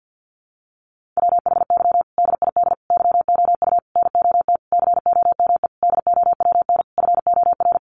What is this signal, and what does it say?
Morse code, heard as a keyed tone, about 50 wpm, 700 Hz. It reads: WHY DID YOU NOT COME DOWN FOR
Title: Sentences from Top 1500 Words 50wpm